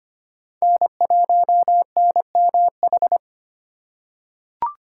Morse code, heard as a keyed tone, about 25 wpm, 700 Hz. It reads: N1NMH
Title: USA Call Signs - 5 Characters - 4x Character Spacing - 25wpm